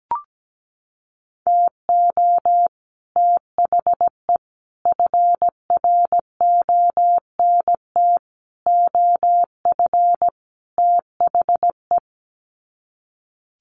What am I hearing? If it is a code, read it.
Morse code, heard as a keyed tone, about 17 wpm, 700 Hz. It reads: TO THE FRONT OF THE
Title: Sets of 5 Words 17wpm